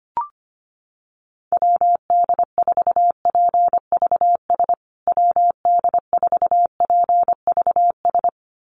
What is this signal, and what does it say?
Morse code, heard as a keyed tone, about 25 wpm, 700 Hz. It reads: WD4PVS WD4PVS
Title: USA Call Signs Repeated 2x 25wpm